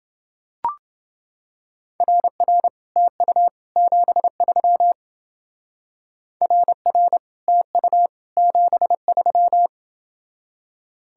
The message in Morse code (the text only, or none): RR TU 73 RR TU 73